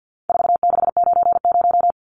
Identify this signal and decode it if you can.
Morse code, heard as a keyed tone, about 50 wpm, 700 Hz. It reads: 4690
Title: Numbers - 4-Digit 50wpm